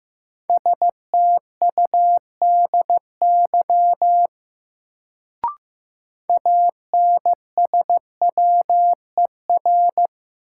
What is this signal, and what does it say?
Morse code, heard as a keyed tone, about 15 wpm, 700 Hz. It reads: STUDY ANSWER
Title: Sentences from Top 300 Words - New Word Drill - 15wpm